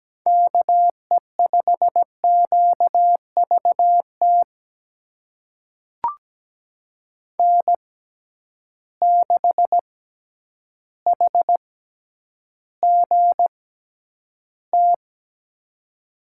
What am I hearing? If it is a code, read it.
Morse code, heard as a keyed tone, about 17 wpm, spaced as for 9 wpm, 700 Hz. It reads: KE5QVT N6HGT
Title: USA Call Signs - 6x Character Spacing - 17wpm